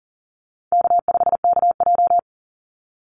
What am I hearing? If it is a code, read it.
Morse code, heard as a keyed tone, about 40 wpm, 700 Hz. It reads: K5KJ
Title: Call Signs - 40wpm